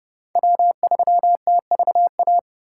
Morse code, heard as a keyed tone, about 28 wpm, 700 Hz. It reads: W3TVA